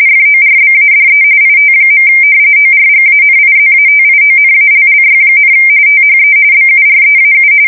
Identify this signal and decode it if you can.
RTTY, about 45.45 baud. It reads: FORBADE IT. ORGANIZATIONALLY, THE BUREAU
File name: fsk170 / usnavy4